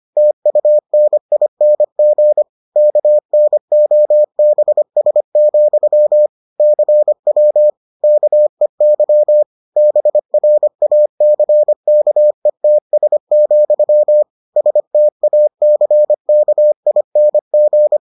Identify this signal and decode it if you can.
Morse code, heard as a keyed tone, about 25 wpm, 600 Hz. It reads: TUNING KNOBS, CW KEY BRACKETS, STACKING